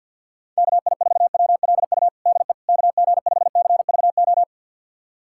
Morse code, heard as a keyed tone, about 53 wpm, 690 Hz. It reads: KI4WLU DE KG5CVO
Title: USA Callsign DE Callsign - 6 Characters - 50wpm